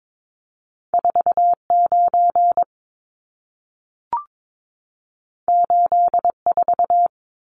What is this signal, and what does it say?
Morse code, heard as a keyed tone, about 22 wpm, 700 Hz. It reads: 49 84
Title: Numbers - 2-Digit 22wpm